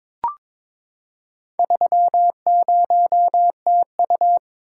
Morse code, heard as a keyed tone, about 22 wpm, 700 Hz. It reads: 30TU
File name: practice / 22wpm / Four Letter-Number Combinations - 22wpm